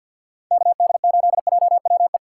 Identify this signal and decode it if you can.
Morse code, heard as a keyed tone, about 50 wpm, 700 Hz. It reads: KD8JWE
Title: USA Call Signs - 4x Character Spacing - 50wpm